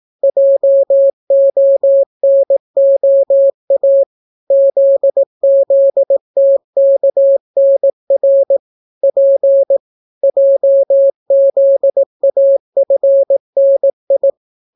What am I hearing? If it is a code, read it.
Morse code, heard as a keyed tone, about 18 wpm, 550 Hz. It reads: JONOA ZZTKNR P JZAFNI